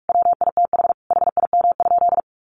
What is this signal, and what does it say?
Morse code, heard as a keyed tone, about 44 wpm, 700 Hz. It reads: WITH HIM?
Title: Sentences from Top 700 Words 45wpm